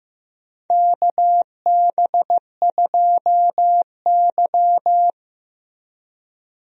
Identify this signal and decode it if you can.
Morse code, heard as a keyed tone, about 15 wpm, 700 Hz. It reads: KB2Y
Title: USA Call Signs - 4 Characters - Repeated 3x - 15wpm